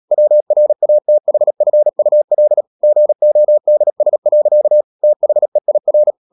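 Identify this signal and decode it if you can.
Morse code, heard as a keyed tone, about 37 wpm, 600 Hz. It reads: WRATHFUL GODS. THEIR
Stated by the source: Morse code practice at 35–39 wpm sending English text of Ken Follett's novel Never, from CW med Gnister.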